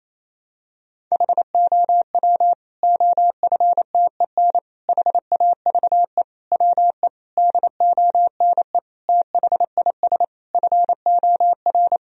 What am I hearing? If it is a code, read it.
Morse code, heard as a keyed tone, about 28 wpm, 700 Hz. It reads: HOW OFTEN HAVE WE DONE THIS FOR
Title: Sentences from Top 1000 Words - Review Part 1 - 28wpm